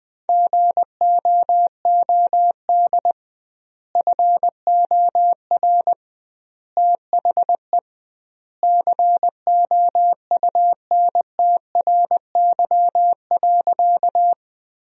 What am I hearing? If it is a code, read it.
Morse code, heard as a keyed tone, about 21 wpm, 700 Hz. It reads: GOOD FOR THE COUNTRY.
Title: 2x Word-Spacing - Sentences from Top 300 Words - 20wpm